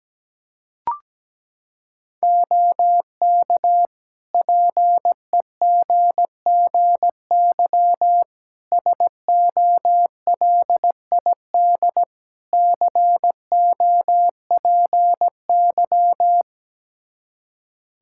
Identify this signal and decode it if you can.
Morse code, heard as a keyed tone, about 17 wpm, 700 Hz. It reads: OK PEGGY SOLID COPY
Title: Practice QSO - Exchange 3 - More Info - 17wpm